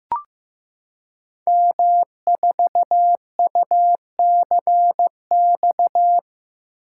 Morse code, heard as a keyed tone, about 15 wpm, 700 Hz. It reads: M4UCX
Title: Code Groups - 5 Letter:Numbers - 15wpm